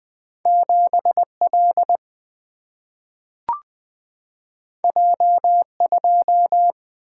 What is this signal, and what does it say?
Morse code, heard as a keyed tone, about 20 wpm, 700 Hz. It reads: 7 L J 2
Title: Two Letter-Number Combinations - 20wpm